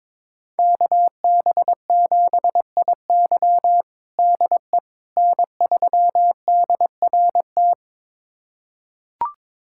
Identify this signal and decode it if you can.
Morse code, heard as a keyed tone, about 22 wpm, 700 Hz. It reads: KB7IY DE N3DRT
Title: USA Callsign DE Callsign - 5 Characters - 22wpm